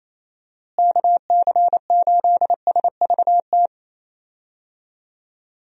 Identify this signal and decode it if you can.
Morse code, heard as a keyed tone, about 27 wpm, 700 Hz. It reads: KC8SVT
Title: USA Call Signs - 6 Characters - Repeated 3x - 28wpm